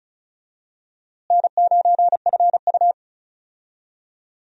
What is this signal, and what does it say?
Morse code, heard as a keyed tone, about 36 wpm, 700 Hz. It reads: N9FU
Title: USA Call Signs - 4 Characters - Repeated 3x - 35wpm